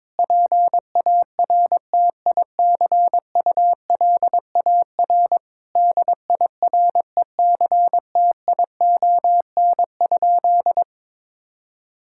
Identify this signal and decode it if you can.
Morse code, heard as a keyed tone, about 22 wpm, 700 Hz. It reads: PARTICULAR DIRECTION?
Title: Sentences from Top 1000 Words - Review Part 5 - 22wpm